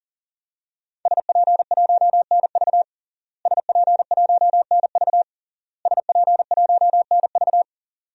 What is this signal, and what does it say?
Morse code, heard as a keyed tone, about 37 wpm, 700 Hz. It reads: SP1NV SP1NV SP1NV
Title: DXCC-100 Call Signs - 5 Characters - Repeated 3x - 40wpm